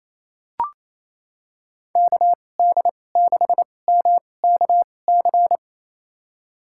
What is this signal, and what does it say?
Morse code, heard as a keyed tone, about 28 wpm, spaced as for 20 wpm, 700 Hz. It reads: KD6MKC
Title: USA Call Signs - 6 Characters - 2x Character Spacing - 28wpm